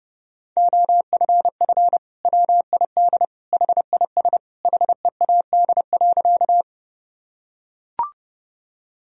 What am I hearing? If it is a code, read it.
Morse code, heard as a keyed tone, about 30 wpm, 700 Hz. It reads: OFF WID HIS HEAD.
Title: Sentences from Top 300 Words and Abbreviations 30wpm